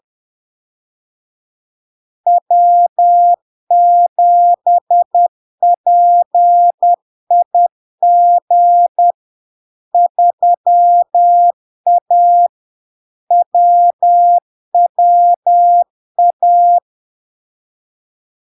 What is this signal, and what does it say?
Morse code, heard as a keyed tone, about 10 wpm, 700 Hz. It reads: W7PIG 3A WWA
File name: LARK-FD-2022-Log-in-CW-10wpm-700-Hz